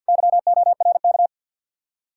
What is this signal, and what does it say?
Morse code, heard as a keyed tone, about 60 wpm, 700 Hz. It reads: YORK
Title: Speed-Racing Common QSO Phrases Encore - 50wpm